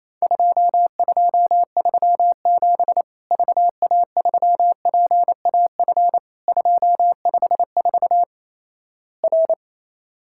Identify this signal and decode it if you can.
Morse code, heard as a keyed tone, about 28 wpm, 700 Hz. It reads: 2237 VA3PAF 254 R
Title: Worked All Europe DX Contest - QTC Exchange - 28wpm